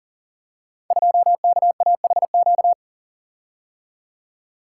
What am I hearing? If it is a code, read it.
Morse code, heard as a keyed tone, about 41 wpm, 700 Hz. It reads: 2KAHQ
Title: Code Groups - 5 Letter:Numbers - 40wpm